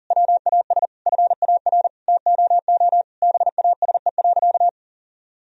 Morse code, heard as a keyed tone, about 40 wpm, 700 Hz. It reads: WAS FAR TOO BASE.
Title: Sentences from Top 1000 Words - Review Part 1 - 40wpm